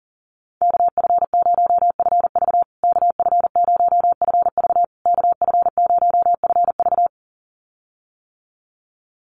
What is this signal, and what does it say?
Morse code, heard as a keyed tone, about 40 wpm, 700 Hz. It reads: KF0FV KF0FV KF0FV
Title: USA Call Signs Repeated 3x 40wpm